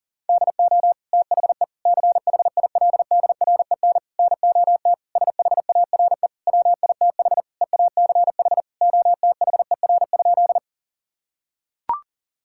Morse code, heard as a keyed tone, about 37 wpm, 700 Hz. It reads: DO THE CHILDREN NOT SHARE WITH EACH OTHER?
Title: Sentences from Top 1000 Words - Review Part 6 - 40wpm